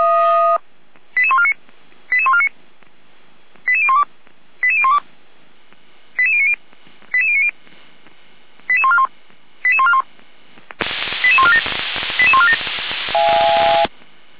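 Five-tone selective call (ZVEI), heard as quick runs of five tones, tones of about 70 ms. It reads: 88149 88149 88012 88012 88090 88090 88141 88141 88147 88147
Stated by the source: ZVEI five-tone call 88501 received on an SDR, fire brigade Feuerwehr Orlamünde.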